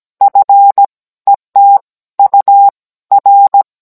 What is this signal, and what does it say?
Morse code, heard as a keyed tone, about 17 wpm, spaced as for 14 wpm, 800 Hz. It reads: F ET UR